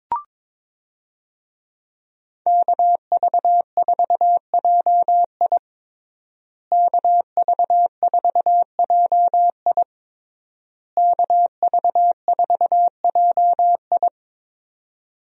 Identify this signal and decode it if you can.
Morse code, heard as a keyed tone, about 22 wpm, 700 Hz. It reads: KV4JI KV4JI KV4JI
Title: USA Call Signs - 5 Characters - Repeated 3x - 22wpm